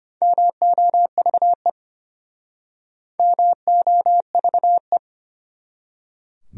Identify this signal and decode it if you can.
Morse code, heard as a keyed tone, about 33 wpm, 700 Hz. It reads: MOVE MOVE